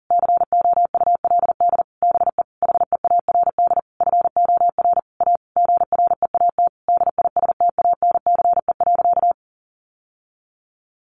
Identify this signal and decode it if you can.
Morse code, heard as a keyed tone, about 40 wpm, 700 Hz. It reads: COULD BE HEARD FOR A GREAT DISTANCE.
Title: Sentences from Top 1200 Words 40wpm